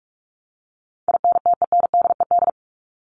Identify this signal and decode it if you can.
Morse code, heard as a keyed tone, about 44 wpm, 700 Hz. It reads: INTENDED
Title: ED Suffix - Top 100 Words with Base Words 45wpm